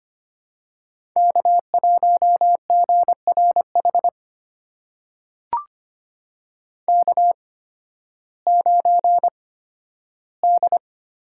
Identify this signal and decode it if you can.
Morse code, heard as a keyed tone, about 25 wpm, spaced as for 13 wpm, 700 Hz. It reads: K1GRH K9D
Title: USA Call Signs - 5 Characters - 8x Character Spacing - 25wpm